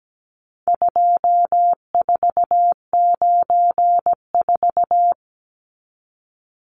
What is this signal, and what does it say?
Morse code, heard as a keyed tone, about 17 wpm, 700 Hz. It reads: 2494
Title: Numbers - 4-Digit 17wpm